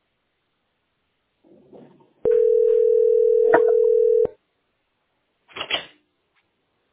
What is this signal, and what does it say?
Ringback tone (North America), 440 + 480 Hz.